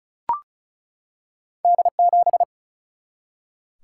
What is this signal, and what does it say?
Morse code, heard as a keyed tone, about 36 wpm, 700 Hz. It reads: D7
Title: Single-Letter Single-Number 35wpm